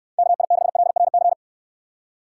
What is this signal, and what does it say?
Morse code, heard as a keyed tone, about 69 wpm, 700 Hz. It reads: KI6LRC